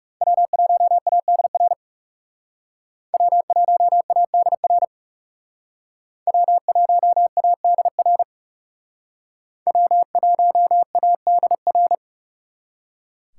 Morse code, heard as a keyed tone, about 37 wpm, 700 Hz. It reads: W1ADR W1ADR W1ADR W1ADR
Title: Speed Racing - USA 1x3 Call Signs 30wpm